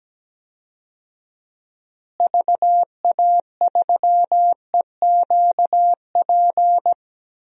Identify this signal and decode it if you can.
Morse code, heard as a keyed tone, about 17 wpm, 700 Hz. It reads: VA3EQP
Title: DXCC-100 Call Signs - 6 Characters - Repeated 3x - 17wpm